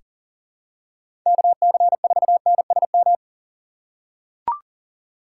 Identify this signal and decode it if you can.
Morse code, heard as a keyed tone, about 40 wpm, 700 Hz. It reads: KC4NSM
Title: USA Call Signs - 6 Characters - 3x Character Spacing - 40wpm